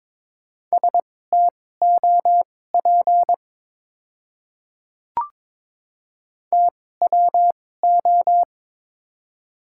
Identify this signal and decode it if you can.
Morse code, heard as a keyed tone, about 22 wpm, spaced as for 16 wpm, 690 Hz. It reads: STOP TWO
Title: 2x Character Spacing - Top 300 Words - 22wpm